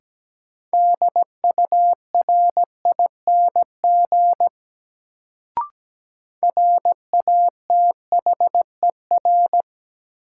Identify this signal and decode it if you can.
Morse code, heard as a keyed tone, about 17 wpm, 700 Hz. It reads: DURING RATHER
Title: Sets of 2 Words - New Word Drill - 17wpm